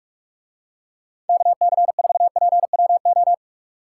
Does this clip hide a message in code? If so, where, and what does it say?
Yes: Morse code from 1.3 to 3.3 s, heard as a keyed tone, about 43 wpm, 700 Hz. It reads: KC4PWO